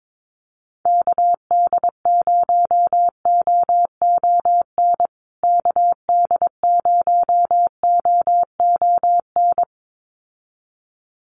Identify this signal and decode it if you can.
Morse code, heard as a keyed tone, about 22 wpm, 700 Hz. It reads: KD0OON KD0OON
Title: USA Call Signs Repeated 2x 22wpm